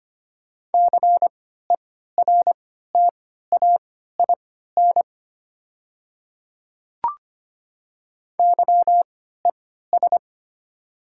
Morse code, heard as a keyed tone, about 25 wpm, spaced as for 14 wpm, 700 Hz. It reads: CERTAIN YES
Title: 3x Character Spacing - Top 500 Words - 25wpm